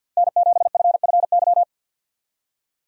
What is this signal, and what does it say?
Morse code, heard as a keyed tone, about 52 wpm, 700 Hz. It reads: N7FFY
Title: USA Call Signs - 5 Characters - 50wpm